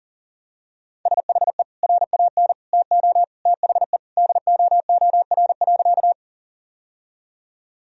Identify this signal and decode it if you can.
Morse code, heard as a keyed tone, about 37 wpm, 700 Hz. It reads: SHE RAN TO THE DOOR.
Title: Sentences from Top 1000 Words - Review Part 3 - 40wpm